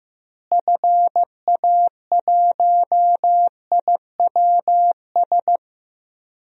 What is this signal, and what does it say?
Morse code, heard as a keyed tone, about 15 wpm, 700 Hz. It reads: FA1IWS
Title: DXCC-100 Call Signs - 6 Characters - Repeated 3x - 15wpm